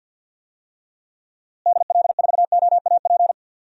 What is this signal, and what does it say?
Morse code, heard as a keyed tone, about 48 wpm, 700 Hz. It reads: DL4OAP
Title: USA Call Signs - 6 Characters - Repeated 3x - 50wpm